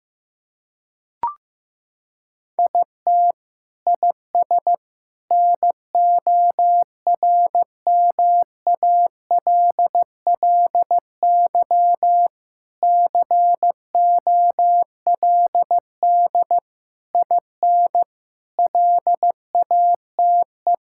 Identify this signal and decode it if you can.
Morse code, heard as a keyed tone, about 15 wpm, 700 Hz. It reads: IT IS NORMALLY COLD IN LATE
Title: Sentences from Top 1000 Words - Review Part 6 - 15wpm